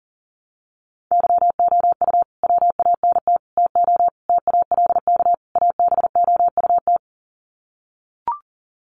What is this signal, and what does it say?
Morse code, heard as a keyed tone, about 40 wpm, 700 Hz. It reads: YOU WANT TO TALK ABOUT
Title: Sets of 5 Words Encore 40wpm